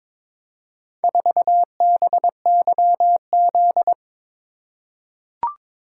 Morse code, heard as a keyed tone, about 23 wpm, 700 Hz. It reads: 4BYZ
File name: practice / 22wpm / Four Letter-Number Combinations - 22wpm